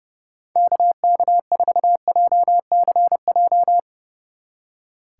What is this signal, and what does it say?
Morse code, heard as a keyed tone, about 38 wpm, 700 Hz. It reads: KK4JCJ